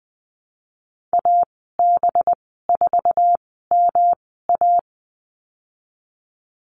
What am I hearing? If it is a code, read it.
Morse code, heard as a keyed tone, about 20 wpm, spaced as for 15 wpm, 700 Hz. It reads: AB4MA
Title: 193 Lesson 31g - Letter v - Call Signs 20wpm